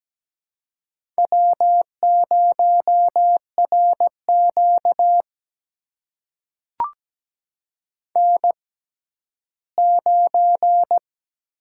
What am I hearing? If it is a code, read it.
Morse code, heard as a keyed tone, about 17 wpm, spaced as for 12 wpm, 700 Hz. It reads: W0RQ N9
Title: USA Call Signs - 4 Characters - 6x Character Spacing - 17wpm